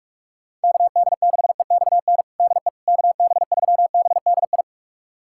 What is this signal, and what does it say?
Morse code, heard as a keyed tone, about 41 wpm, 700 Hz. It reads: KD6EXN DE KB3BDI
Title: USA Callsign DE Callsign 45wpm